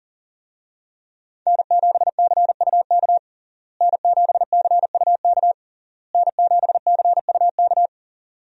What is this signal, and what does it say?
Morse code, heard as a keyed tone, about 40 wpm, 700 Hz. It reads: N7CUK N7CUK N7CUK
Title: USA Call Signs - 5 Characters - Repeated 3x - 40wpm